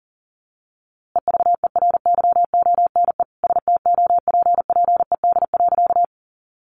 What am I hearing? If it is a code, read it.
Morse code, heard as a keyed tone, about 40 wpm, 700 Hz. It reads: EVERYONE STOPPED.
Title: A Sound Of Thunder 40wpm